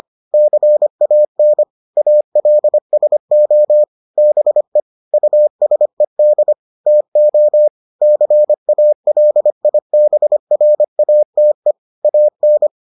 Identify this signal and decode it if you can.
Morse code, heard as a keyed tone, about 25 wpm, 600 Hz. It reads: CAN ALSO BE USED TO CALIBRATE AN